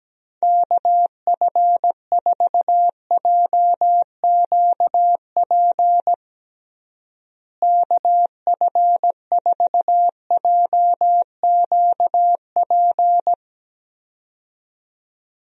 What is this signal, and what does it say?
Morse code, heard as a keyed tone, about 17 wpm, 700 Hz. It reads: KF4JQP KF4JQP
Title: USA Call Signs - 6 Characters - Repeated 2x - 17wpm